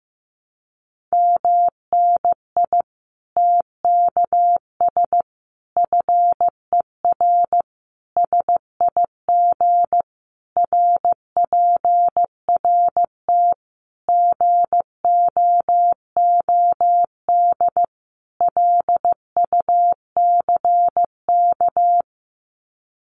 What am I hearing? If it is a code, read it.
Morse code, heard as a keyed tone, about 15 wpm, 700 Hz. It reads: MNI TKS FER SIG RPRT GOOD LUCK
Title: Common QSO Phrases 15wpm